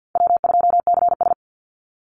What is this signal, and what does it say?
Morse code, heard as a keyed tone, about 49 wpm, 700 Hz. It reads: W2CS
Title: Call Signs Repeated 3x - 50wpm